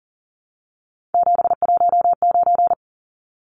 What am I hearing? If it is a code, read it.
Morse code, heard as a keyed tone, about 40 wpm, 700 Hz. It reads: 719
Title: Numbers - 3-Digit 40wpm